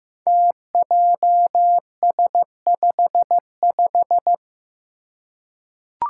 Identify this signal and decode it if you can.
Morse code, heard as a keyed tone, about 15 wpm, 700 Hz. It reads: TJS55
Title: Five Challenging Letter-Number Combinations 15wpm